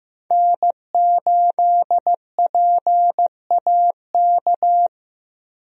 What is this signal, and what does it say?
Morse code, heard as a keyed tone, about 15 wpm, 700 Hz. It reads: N8PAK
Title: USA Call Signs - 5 Characters - 15wpm